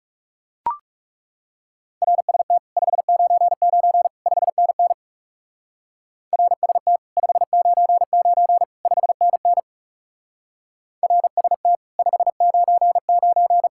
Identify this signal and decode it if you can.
Morse code, heard as a keyed tone, about 36 wpm, 700 Hz. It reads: RST 599 5NN RST 599 5NN RST 599
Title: Speed-Racing Common QSO Phrases - 30wpm